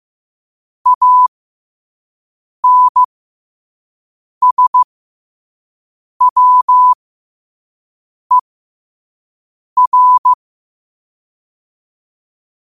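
Morse code, heard as a keyed tone, about 15 wpm, spaced as for 5 wpm, 1000 Hz. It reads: ANSWER
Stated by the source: Morse code practice at 5 wpm.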